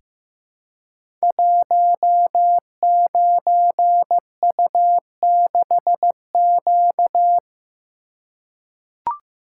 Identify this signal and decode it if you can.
Morse code, heard as a keyed tone, about 15 wpm, 700 Hz. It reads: 19U6Q
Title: Code Groups - 5 Letter:Numbers - 15wpm